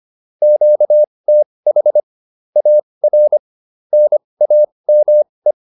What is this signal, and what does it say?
Morse code, heard as a keyed tone, about 25 wpm, spaced as for 20 wpm, 600 Hz. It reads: QTH AR NAME